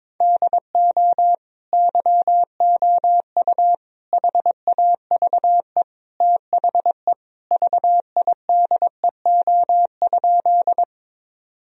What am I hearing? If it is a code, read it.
Morse code, heard as a keyed tone, about 22 wpm, 700 Hz. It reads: DO YOU HAVE THE VIDEO?
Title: Sentences from Top 1000 Words - Review Part 4 - 22wpm